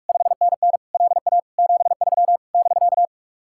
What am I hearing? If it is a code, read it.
Morse code, heard as a keyed tone, about 60 wpm, spaced as for 52 wpm, 700 Hz. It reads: 5NN LA 73 <BK>
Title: Speed-Racing Common QSO Phrases Encore - 45wpm